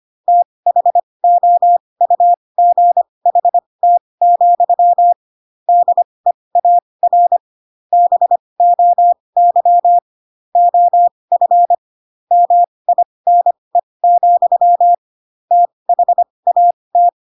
Morse code, heard as a keyed tone, about 25 wpm, spaced as for 20 wpm, 700 Hz. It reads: THOUGHT, DEAR BOY OF MINE, THAT
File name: Pinocchio / Pinocchio ch7 15 25 ik1hge